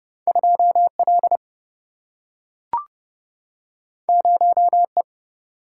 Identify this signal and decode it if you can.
Morse code, heard as a keyed tone, about 30 wpm, 700 Hz. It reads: 2 L 0 E
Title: Single-Letter Single-Number 30wpm